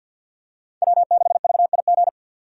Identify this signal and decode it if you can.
Morse code, heard as a keyed tone, about 48 wpm, 700 Hz. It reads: W6VIG
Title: USA Call Signs - 5 Characters - 2x Character Spacing - 50wpm